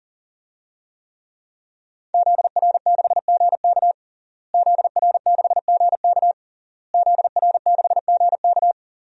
Morse code, heard as a keyed tone, about 37 wpm, 700 Hz. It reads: ZR6GK ZR6GK ZR6GK
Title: DXCC-100 Call Signs - 5 Characters - Repeated 3x - 40wpm